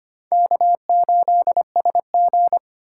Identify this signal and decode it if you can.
Morse code, heard as a keyed tone, about 33 wpm, 700 Hz. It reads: K8SG